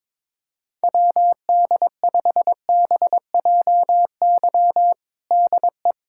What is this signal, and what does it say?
Morse code, heard as a keyed tone, about 22 wpm, 700 Hz. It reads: WD5BJY DE
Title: USA Callsign DE Callsign - 6 Characters - 22wpm